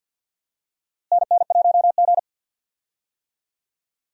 Morse code, heard as a keyed tone, about 52 wpm, 700 Hz. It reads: NN1G
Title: USA Call Signs - 4 Characters - Repeated 3x - 50wpm